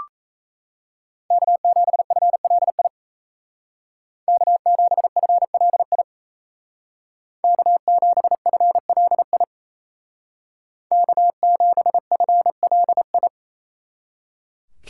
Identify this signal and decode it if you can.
Morse code, heard as a keyed tone, about 36 wpm, 700 Hz. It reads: K7FLI K7FLI K7FLI K7FLI